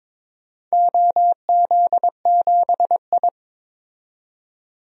Morse code, heard as a keyed tone, about 22 wpm, 700 Hz. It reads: OZ7I
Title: DXCC-100 Call Signs - 4 Characters - 22wpm